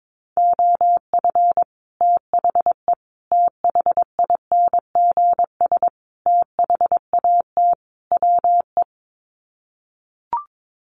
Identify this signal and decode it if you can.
Morse code, heard as a keyed tone, about 22 wpm, 700 Hz. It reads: OF THE THINGS THAT WE
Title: Sets of 5 Words Encore 22wpm